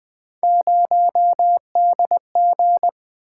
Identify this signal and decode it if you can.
Morse code, heard as a keyed tone, about 24 wpm, 700 Hz. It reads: 0DG
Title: Speed-Racing - Three Letter-Number Combinations - 20wpm